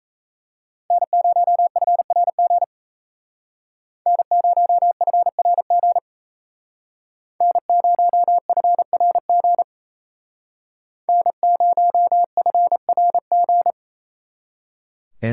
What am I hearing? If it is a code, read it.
Morse code, heard as a keyed tone, about 35 wpm, 700 Hz. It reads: N0FRG N0FRG N0FRG N0FRG
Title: Speed Racing - USA 1x3 Call Signs 28wpm